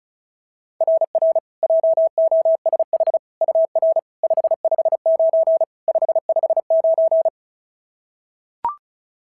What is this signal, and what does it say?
Morse code, heard as a keyed tone, about 36 wpm, 690 Hz. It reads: RR JOSH UR 559 559
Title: SOTA QSO Practice 35wpm